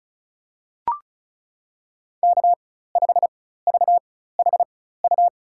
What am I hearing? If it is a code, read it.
Morse code, heard as a keyed tone, about 36 wpm, spaced as for 16 wpm, 700 Hz. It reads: K5VHU
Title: USA Call Signs - 5 Characters - 4x Character Spacing - 35wpm